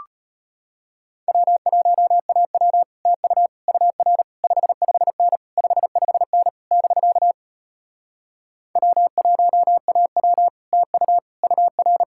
Morse code, heard as a keyed tone, about 33 wpm, 700 Hz. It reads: W1AW TU UR 55N 55N <BK> W1AW TU UR